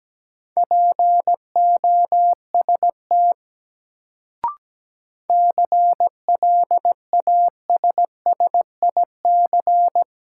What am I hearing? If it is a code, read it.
Morse code, heard as a keyed tone, about 17 wpm, 700 Hz. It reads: POST CLASSIC
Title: Sentences from Top 1000 Words - New Word Drill - 17wpm